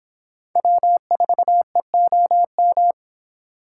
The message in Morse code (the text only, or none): W4EOM